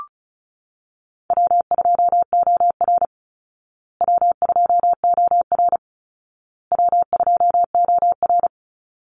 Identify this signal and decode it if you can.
Morse code, heard as a keyed tone, about 34 wpm, 700 Hz. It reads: W2OR W2OR W2OR
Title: Easy Call Signs Repeated 3x - 35wpm